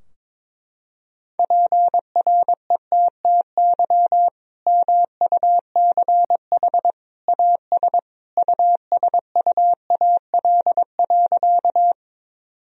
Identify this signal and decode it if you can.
Morse code, heard as a keyed tone, about 22 wpm, 700 Hz. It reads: PRETTY MUCH AS USUAL.
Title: Sentences from Top 1000 Words - Review Part 2 - 22wpm